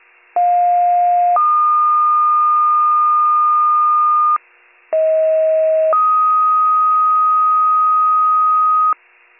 Two-tone page, heard as a one-second tone, then a three-second tone, about 710 then 1220 Hz, about 640 then 1250 Hz. It is heard from 0.4 to 8.9 s.